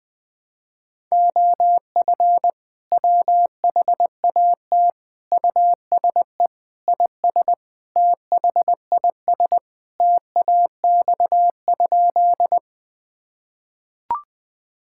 Morse code, heard as a keyed tone, about 20 wpm, 700 Hz. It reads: OF WHAT USE IS THIS TAX?
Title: Sentences from Top 1000 Words - Review Part 5 - 20wpm